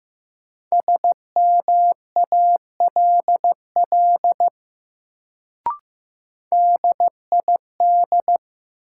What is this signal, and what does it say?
Morse code, heard as a keyed tone, about 15 wpm, 700 Hz. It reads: SMALL DID
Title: Sets of 2 Words - New Word Drill - 15wpm